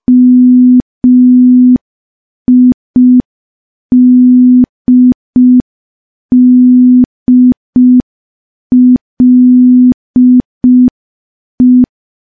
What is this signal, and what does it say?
Morse code, heard as a keyed tone, about 5 wpm, 250 Hz. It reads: MIDDLE
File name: morse / morse-training-05WPM